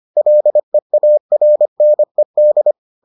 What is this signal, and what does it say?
Morse code, heard as a keyed tone, about 25 wpm, 600 Hz. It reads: LEARNED